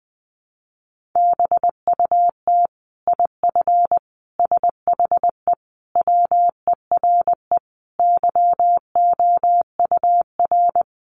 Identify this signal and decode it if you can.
Morse code, heard as a keyed tone, about 19 wpm, 700 Hz. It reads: BUT IF SHE WERE YOUR
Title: Sentences from Top 1500 Words 20wpm